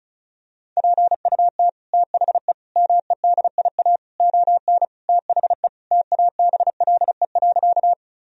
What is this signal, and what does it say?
Morse code, heard as a keyed tone, about 36 wpm, 700 Hz. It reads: PUT THE MEDIA ON THE TABLE.
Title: Sentences from Top 1000 Words - Review Part 4 - 35wpm